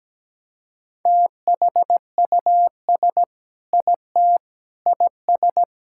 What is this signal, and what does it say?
Morse code, heard as a keyed tone, about 17 wpm, 700 Hz. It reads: THUS IT IS
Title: Sentences from Top 1000 Words - Review Part 5 - 17wpm